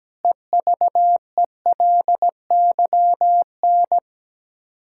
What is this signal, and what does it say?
Morse code, heard as a keyed tone, about 17 wpm, 700 Hz. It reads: EVELYN
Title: Practice QSO - Exchange 1 - Beginning - 17wpm